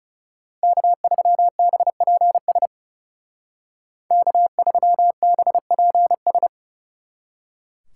Morse code, heard as a keyed tone, about 39 wpm, 700 Hz. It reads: K3BPS K3BPS